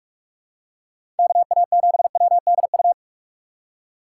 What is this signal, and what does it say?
Morse code, heard as a keyed tone, about 46 wpm, 700 Hz. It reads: KA7WDU
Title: USA Call Signs - 6 Characters - 6x Character Spacing - 45wpm